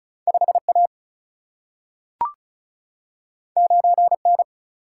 Morse code, heard as a keyed tone, about 36 wpm, 700 Hz. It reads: 5A 9N
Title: Single-Letter Single-Number 35wpm